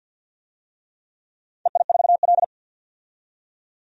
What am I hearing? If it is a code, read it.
Morse code, heard as a keyed tone, about 52 wpm, 700 Hz. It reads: EI4L
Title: DXCC-100 Call Signs - 4 Characters - 50wpm